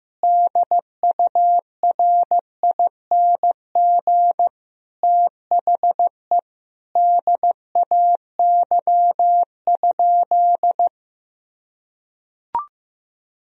Morse code, heard as a keyed tone, about 15 wpm, 700 Hz. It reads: DURING THE DAY?
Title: Sentences from Top 1000 Words - Review Part 2 - 15wpm